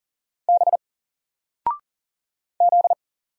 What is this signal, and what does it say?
Morse code, heard as a keyed tone, about 40 wpm, 700 Hz. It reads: B Z
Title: Single Letter-Number Alt 40wpm